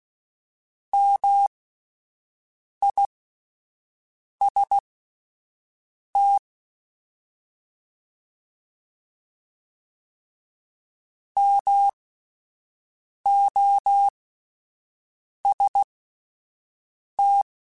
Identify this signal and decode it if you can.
Morse code, heard as a keyed tone, about 16 wpm, spaced as for 5 wpm, 800 Hz. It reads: MIST MOST